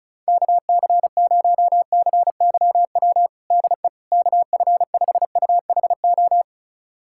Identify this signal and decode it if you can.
Morse code, heard as a keyed tone, about 36 wpm, 700 Hz. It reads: KC0CYW DE KF5UHO
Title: USA Callsign DE Callsign - 6 Characters - 35wpm